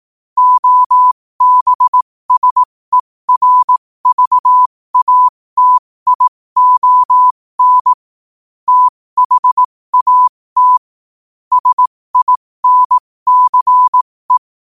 Morse code, heard as a keyed tone, about 18 wpm, spaced as for 15 wpm, 1000 Hz. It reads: OBSERVATION THAT SINCE